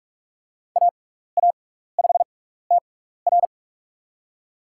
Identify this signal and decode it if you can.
Morse code, heard as a keyed tone, about 44 wpm, spaced as for 15 wpm, 700 Hz. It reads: AA5TR
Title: USA Call Signs - 6x Character Spacing - 45wpm